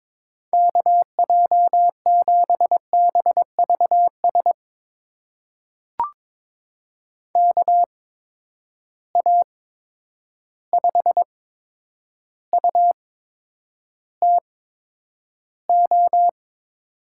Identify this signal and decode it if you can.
Morse code, heard as a keyed tone, about 22 wpm, spaced as for 9 wpm, 700 Hz. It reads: KJ7BVS KA5UTO
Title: USA Call Signs - 6 Characters - 8x Character Spacing - 22wpm